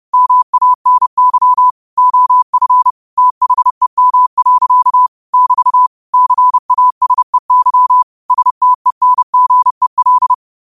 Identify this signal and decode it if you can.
Morse code, heard as a keyed tone, about 30 wpm, 1000 Hz. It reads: MANY OF THEM. <BT> CASEY STENGEL